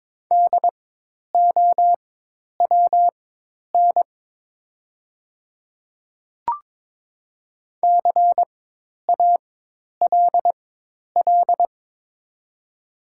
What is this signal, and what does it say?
Morse code, heard as a keyed tone, about 22 wpm, spaced as for 10 wpm, 700 Hz. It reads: DOWN CALL
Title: 4x Character Spacing - Top 100 Words - 22wpm